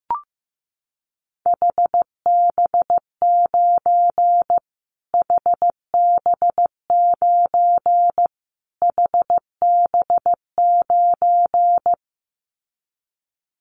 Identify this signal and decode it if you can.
Morse code, heard as a keyed tone, about 15 wpm, 700 Hz. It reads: HB9 HB9 HB9
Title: Callsign Prefixes for Top 25 DXCC - Repeated 3x - 15wpm